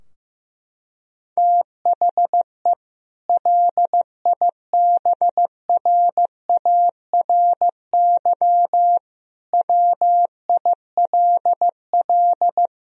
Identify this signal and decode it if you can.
Morse code, heard as a keyed tone, about 15 wpm, 700 Hz. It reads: THE LIBRARY WILL
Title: Sentences from Top 1000 Words - Review Part 4 - 15wpm